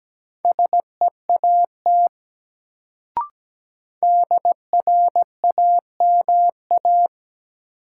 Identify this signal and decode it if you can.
Morse code, heard as a keyed tone, about 17 wpm, 700 Hz. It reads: SEAT DRAMA
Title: Sentences from Top 1500 Words - New Word Drill - 17wpm